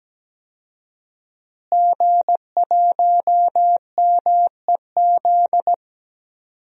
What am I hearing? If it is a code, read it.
Morse code, heard as a keyed tone, about 17 wpm, 700 Hz. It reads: G1MEZ
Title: DXCC-100 Call Signs - 5 Characters - Repeated 2x - 17wpm